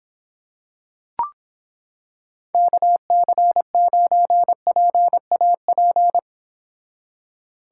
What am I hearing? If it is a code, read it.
Morse code, heard as a keyed tone, about 26 wpm, 700 Hz. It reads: KC9PAP